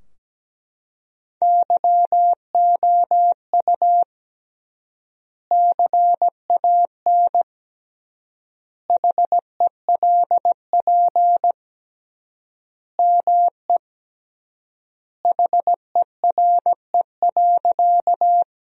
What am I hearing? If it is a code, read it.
Morse code, heard as a keyed tone, about 17 wpm, 700 Hz. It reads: YOU CAN HELP ME HERE.
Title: 3x Word-Spacing - Sentences from Top 200 Words - 17wpm